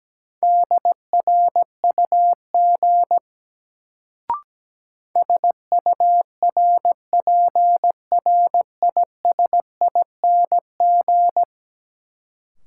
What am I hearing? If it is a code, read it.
Morse code, heard as a keyed tone, about 17 wpm, 700 Hz. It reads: DRUG SURPRISING
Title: Sets of 5 Words - New Word Drill - 17wpm